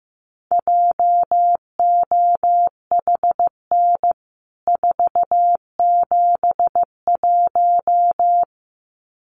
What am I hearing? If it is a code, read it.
Morse code, heard as a keyed tone, about 15 wpm, 700 Hz. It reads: JOHN 471
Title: CWT Exchange 15wpm